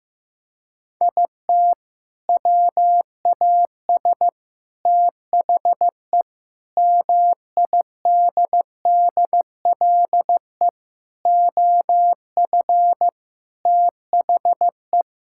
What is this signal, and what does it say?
Morse code, heard as a keyed tone, about 15 wpm, 700 Hz. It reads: IT WAS THE MIDDLE OF THE
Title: Sentences from Top 1000 Words - Review Part 6 - 15wpm